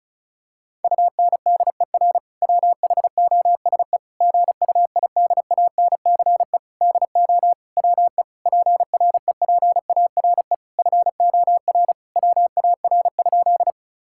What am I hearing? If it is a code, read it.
Morse code, heard as a keyed tone, about 36 wpm, 700 Hz. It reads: UNDER WHOSE GUIDANCE DO WE PREPARE FOR WAR?
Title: Sentences from Top 1600 Words 35wpm